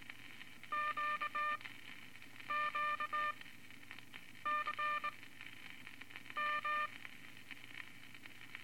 Morse code, heard as a keyed tone, about 18 wpm, spaced as for 7 wpm, 1200 Hz, with noise. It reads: QQCM
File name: Morse Code Records / 13A LCM 154 Units I-IV Groups of Four cont